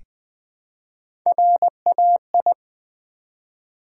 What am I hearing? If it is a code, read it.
Morse code, heard as a keyed tone, about 20 wpm, 700 Hz. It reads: RAI
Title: Build A Word - 1 Letter At A Time - 4-Letter Words - 20wpm